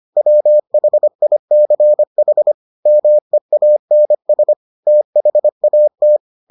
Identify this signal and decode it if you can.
Morse code, heard as a keyed tone, about 25 wpm, 600 Hz. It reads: WHICH MEANS THAT